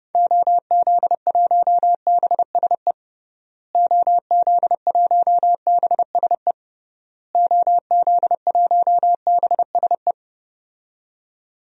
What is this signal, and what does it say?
Morse code, heard as a keyed tone, about 30 wpm, 700 Hz. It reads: OZ1BSE OZ1BSE OZ1BSE
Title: DXCC-100 Call Signs - 6 Characters - Repeated 3x - 30wpm